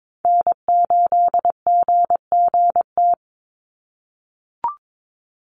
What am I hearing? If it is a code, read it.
Morse code, heard as a keyed tone, about 22 wpm, 700 Hz. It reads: N8GGT
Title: USA Call Signs 22wpm